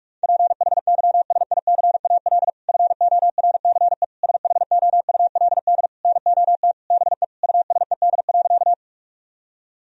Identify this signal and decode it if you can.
Morse code, heard as a keyed tone, about 42 wpm, 700 Hz. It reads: PHYSICAL FORCE SHOULD NOT BE USED.
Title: Sentences from Top 1000 Words - Review Part 4 - 45wpm